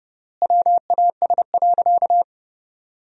Morse code, heard as a keyed tone, about 30 wpm, 700 Hz. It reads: WAS.
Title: 3x Word-Spacing - Sentences from Top 100 Words - 30wpm